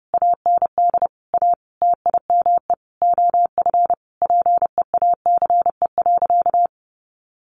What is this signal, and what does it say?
Morse code, heard as a keyed tone, about 30 wpm, 700 Hz. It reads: AND A TIME OF PEACE.
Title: Sentences from Top 1500 Words 30wpm